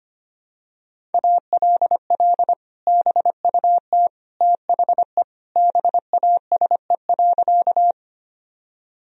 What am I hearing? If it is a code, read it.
Morse code, heard as a keyed tone, about 25 wpm, 700 Hz. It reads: ALL BUT THE BASE.
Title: Sentences from Top 1000 Words - Review Part 1 - 25wpm